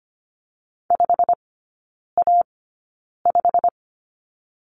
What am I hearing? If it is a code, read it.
Morse code, heard as a keyed tone, about 25 wpm, spaced as for 9 wpm, 700 Hz. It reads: 5A5
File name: Lesson 15e - Number 5 - Three Characters 25wpm